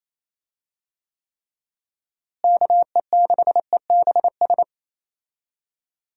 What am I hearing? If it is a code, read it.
Morse code, heard as a keyed tone, about 28 wpm, 700 Hz. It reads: KE6EBS
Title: USA Call Signs - 6 Characters - Repeated 3x - 28wpm